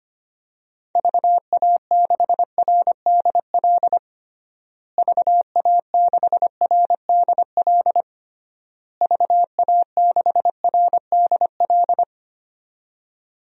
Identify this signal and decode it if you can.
Morse code, heard as a keyed tone, about 25 wpm, 700 Hz. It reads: VA6RDL VA6RDL VA6RDL
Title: DXCC-100 Call Signs - 6 Characters - Repeated 3x - 25wpm